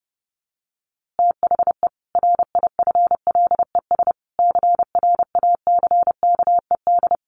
Morse code, heard as a keyed tone, about 28 wpm, 700 Hz. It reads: THE RIFLES CRACKED
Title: A Sound Of Thunder 30wpm